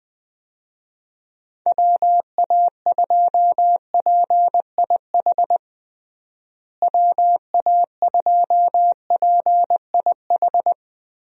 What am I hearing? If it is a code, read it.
Morse code, heard as a keyed tone, about 20 wpm, 700 Hz. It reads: WA2PIH WA2PIH
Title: USA Call Signs - 6 Characters - Repeated 3x - 20wpm